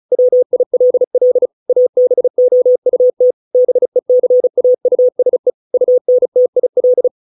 Morse code, heard as a keyed tone, about 35 wpm, 500 Hz. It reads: WILL ABOUT BECAUSE UNTIL